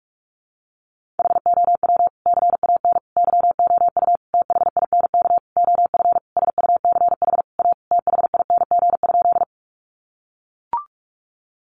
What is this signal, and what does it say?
Morse code, heard as a keyed tone, about 44 wpm, 700 Hz. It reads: HOW CAN YOU THINK OF SUCH A THING?
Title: Sentences from Top 200 Words 45wpm